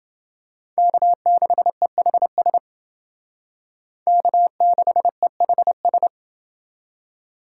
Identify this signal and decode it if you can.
Morse code, heard as a keyed tone, about 31 wpm, 700 Hz. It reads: K6EHS K6EHS